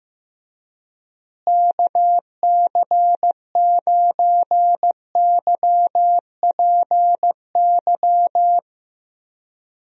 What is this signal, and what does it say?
Morse code, heard as a keyed tone, about 15 wpm, 700 Hz. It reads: KC9YPY
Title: USA Call Signs - 6 Characters - Repeated 2x - 15wpm